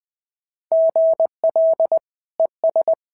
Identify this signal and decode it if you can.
Morse code, heard as a keyed tone, about 20 wpm, 690 Hz. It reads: GL ES
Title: SOTA QSO Practice 20wpm